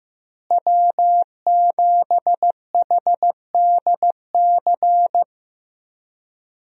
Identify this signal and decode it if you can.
Morse code, heard as a keyed tone, about 15 wpm, 700 Hz. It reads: W7HDC
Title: USA Call Signs - 5 Characters - Repeated 2x - 15wpm